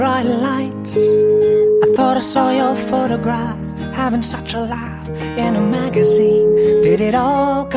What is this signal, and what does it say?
Ringback tone (Europe, 425 Hz), with noise.